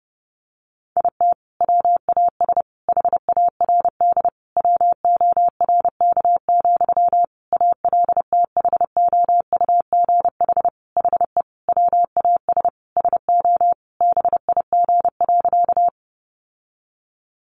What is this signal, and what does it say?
Morse code, heard as a keyed tone, about 30 wpm, 700 Hz. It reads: IT WAS HARD WORK, ALTHOUGH HE WAS SO BIG.
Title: Sentences from Top 600 Words 30wpm